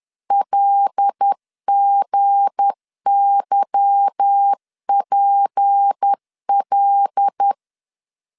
Morse code, heard as a keyed tone, about 10 wpm, 800 Hz. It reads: LGYPL
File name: LEZIONI / 8 YJ45 / 8 Y 10wpm 2615